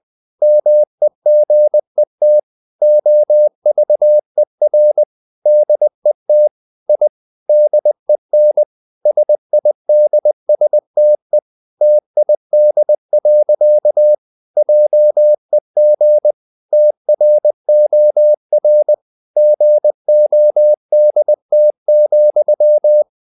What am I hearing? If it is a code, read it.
Morse code, heard as a keyed tone, about 20 wpm, 600 Hz. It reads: MEGET OVER DET I DEN SIDSTE TID. JEG TROR GODT,
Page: Ferie 20 wpm